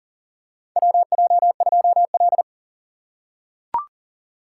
Morse code, heard as a keyed tone, about 37 wpm, 700 Hz. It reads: WJ2L
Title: K1USN SST QSO Practice - 40wpm